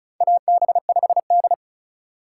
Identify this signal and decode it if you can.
Morse code, heard as a keyed tone, about 36 wpm, 700 Hz. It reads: AB5D
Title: USA Call Signs - 4 Characters - 2x Character Spacing - 35wpm